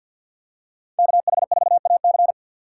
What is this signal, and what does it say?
Morse code, heard as a keyed tone, about 51 wpm, 700 Hz. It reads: KH4AC
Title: USA Call Signs - 5 Characters - 2x Character Spacing - 50wpm